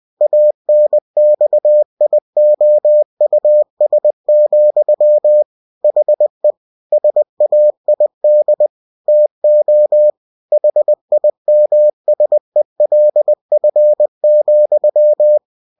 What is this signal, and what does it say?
Morse code, heard as a keyed tone, about 20 wpm, 600 Hz. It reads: ANXIOUS, HE SAID TO HIMSELF,
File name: cw-books / winnie-the-pooh / winnie-20wpm-page-0091